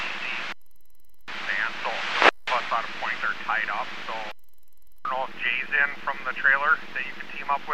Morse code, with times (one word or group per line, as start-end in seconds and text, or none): none